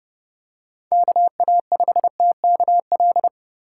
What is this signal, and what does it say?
Morse code, heard as a keyed tone, about 30 wpm, 700 Hz. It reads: KA5TKL
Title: USA Call Signs - 6 Characters - 2x Character Spacing - 30wpm